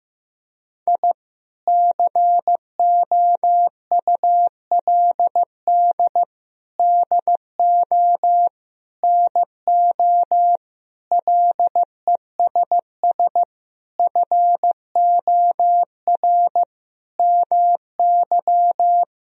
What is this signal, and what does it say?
Morse code, heard as a keyed tone, about 16 wpm, 700 Hz. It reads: I COULD DO NO LESS FOR MY
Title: Sentences from Top 1000 Words - Review Part 2 - 15wpm